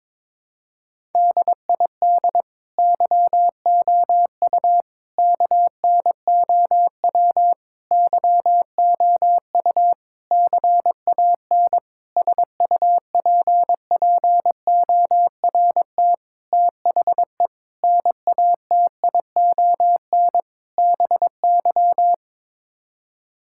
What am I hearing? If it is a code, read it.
Morse code, heard as a keyed tone, about 21 wpm, 700 Hz. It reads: DID YOU KNOW YOU CAN SUPPORT THE NATION BY
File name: Sets of 9 Words - 22wpm